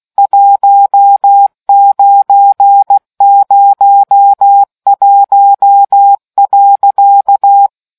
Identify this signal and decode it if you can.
Morse code, heard as a keyed tone, about 16 wpm, 800 Hz. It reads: 1901.